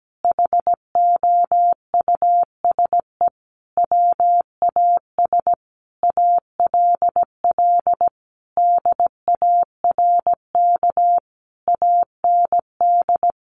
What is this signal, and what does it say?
Morse code, heard as a keyed tone, about 17 wpm, 700 Hz. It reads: HOUSE WAS ALL DARK AND
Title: Sentences from Top 500 Words 17wpm